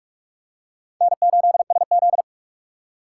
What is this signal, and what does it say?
Morse code, heard as a keyed tone, about 46 wpm, 700 Hz. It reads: N8SZ
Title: USA Call Signs - 4 Characters - 6x Character Spacing - 45wpm